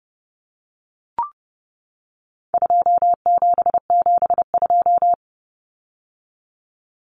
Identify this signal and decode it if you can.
Morse code, heard as a keyed tone, about 30 wpm, 700 Hz. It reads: 2772
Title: Common Cut Numbers - 4-Digit 30wpm